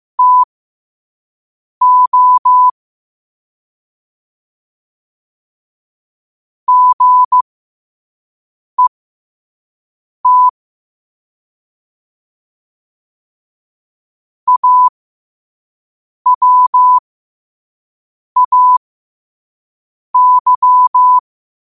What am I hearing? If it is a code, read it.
Morse code, heard as a keyed tone, about 15 wpm, spaced as for 5 wpm, 1000 Hz. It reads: TO GET AWAY
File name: qotdfiles / 5wpm